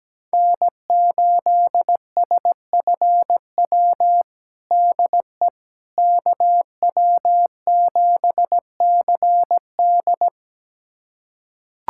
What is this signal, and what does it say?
Morse code, heard as a keyed tone, about 17 wpm, 700 Hz. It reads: N8SFW DE KW7CD
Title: USA Callsign DE Callsign - 5 Characters - 17wpm